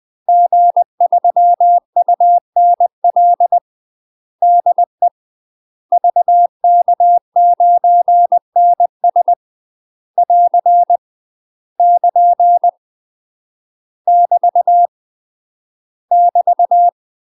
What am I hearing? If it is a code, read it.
Morse code, heard as a keyed tone, about 20 wpm, 700 Hz. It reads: G3UNL DE VK9NS <AR> <KN> <BT> <BT>
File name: UK MASTER QSOs TRAINING / 20WPM / UK MASTER QSOs 0034